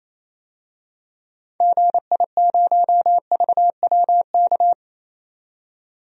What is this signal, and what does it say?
Morse code, heard as a keyed tone, about 28 wpm, 700 Hz. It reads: GI0VWK
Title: DXCC-100 Call Signs - 6 Characters - 28wpm